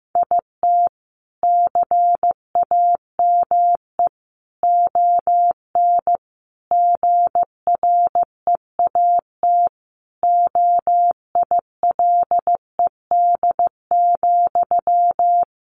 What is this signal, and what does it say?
Morse code, heard as a keyed tone, about 14 wpm, 700 Hz. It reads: IT CAME ON GREAT OILED,
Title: A Sound Of Thunder 15wpm